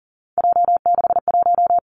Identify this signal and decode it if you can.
Morse code, heard as a keyed tone, about 42 wpm, 700 Hz. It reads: J61
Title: Three Challenging Letter-Number Combinations 40wpm